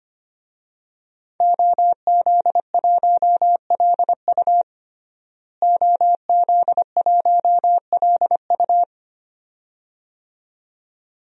Morse code, heard as a keyed tone, about 25 wpm, 700 Hz. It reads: OZ1LU OZ1LU
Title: DXCC-100 Call Signs - 5 Characters - Repeated 2x - 25wpm